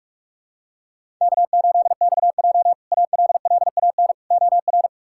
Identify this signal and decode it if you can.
Morse code, heard as a keyed tone, about 45 wpm, 700 Hz. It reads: K8XJ ALLAN OR
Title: K1USN SST QSO Practice - 45wpm